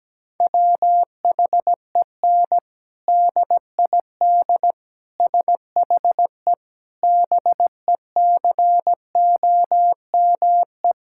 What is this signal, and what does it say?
Morse code, heard as a keyed tone, about 17 wpm, 700 Hz. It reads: WHEN DID SHE BECOME
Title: Sentences from Top 1000 Words - Review Part 6 - 17wpm